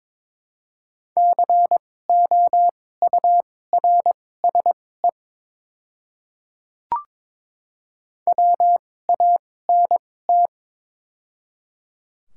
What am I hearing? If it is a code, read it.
Morse code, heard as a keyed tone, about 22 wpm, spaced as for 16 wpm, 700 Hz. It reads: COURSE WANT
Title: 2x Character Spacing - Top 500 Words - 22wpm